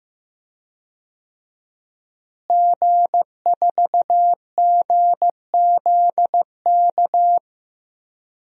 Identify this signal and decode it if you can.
Morse code, heard as a keyed tone, about 15 wpm, 700 Hz. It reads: G4GZK
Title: DXCC-100 Call Signs - 5 Characters - Repeated 2x - 15wpm